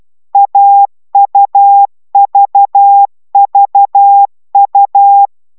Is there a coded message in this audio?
Yes: Morse code from 0.3 to 5.3 s, heard as a keyed tone, about 12 wpm, 800 Hz. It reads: AUVVU